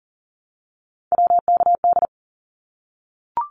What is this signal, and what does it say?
Morse code, heard as a keyed tone, about 40 wpm, 700 Hz. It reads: WKD
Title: Abbreviations Repeated 3x - 40wpm